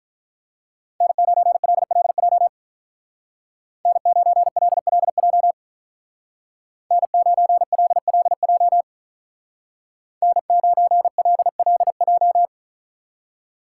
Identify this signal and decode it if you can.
Morse code, heard as a keyed tone, about 43 wpm, 700 Hz. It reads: N9LLJ N9LLJ N9LLJ N9LLJ
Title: Speed Racing - USA 1x3 Call Signs 35wpm